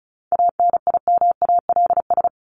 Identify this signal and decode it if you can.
Morse code, heard as a keyed tone, about 34 wpm, 700 Hz. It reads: ANIMALS
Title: ES:S Suffix - Top 100 Words with Base Words 35wpm